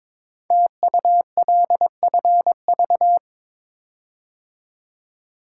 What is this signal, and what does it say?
Morse code, heard as a keyed tone, about 23 wpm, 700 Hz. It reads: TULFV
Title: Code Groups - 5 Letters - 22wpm